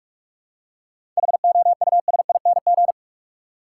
Morse code, heard as a keyed tone, about 46 wpm, 700 Hz. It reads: HOUSING
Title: Seven Letter Words - 45wpm